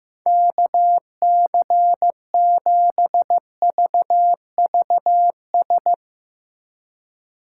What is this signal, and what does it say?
Morse code, heard as a keyed tone, about 15 wpm, 700 Hz. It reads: KC7VVS
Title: Speed Racing - USA 2x3 Call Signs 15wpm